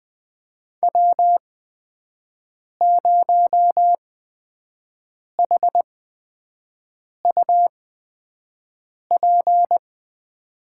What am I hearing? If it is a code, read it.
Morse code, heard as a keyed tone, about 20 wpm, spaced as for 5 wpm, 700 Hz. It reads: W0HUP
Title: USA Call Signs - 5 Characters - 8x Character Spacing - 20wpm